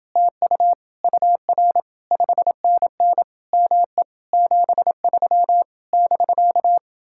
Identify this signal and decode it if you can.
Morse code, heard as a keyed tone, about 28 wpm, 700 Hz. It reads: TU UR 5NN ME 73 <BK>